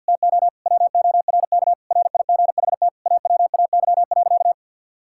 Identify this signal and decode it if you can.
Morse code, heard as a keyed tone, about 50 wpm, 700 Hz. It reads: TO WORK RIGHT AWAY.
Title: Sentences from Top 1600 Words 50wpm